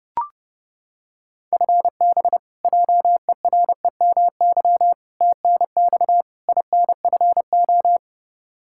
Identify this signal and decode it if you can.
Morse code, heard as a keyed tone, about 30 wpm, 700 Hz. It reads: FB JEREMY TNX INFO
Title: Practice QSO - Exchange 3 - More Info - 30wpm